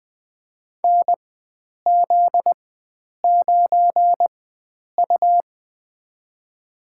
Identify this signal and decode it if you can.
Morse code, heard as a keyed tone, about 20 wpm, spaced as for 9 wpm, 700 Hz. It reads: NZ9U
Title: USA Call Signs - 4 Characters - 4x Character Spacing - 20wpm